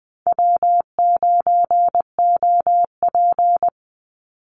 Morse code, heard as a keyed tone, about 20 wpm, 700 Hz. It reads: W9OP
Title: Call Signs Repeated 2x - 20wpm